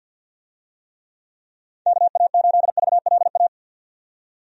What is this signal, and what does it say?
Morse code, heard as a keyed tone, about 50 wpm, 700 Hz. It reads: KA8VLA
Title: USA Call Signs - 6 Characters - 50wpm